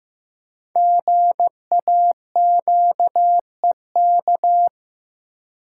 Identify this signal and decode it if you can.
Morse code, heard as a keyed tone, about 15 wpm, 700 Hz. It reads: GAQEK
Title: Code Groups - 5 Letters - 15wpm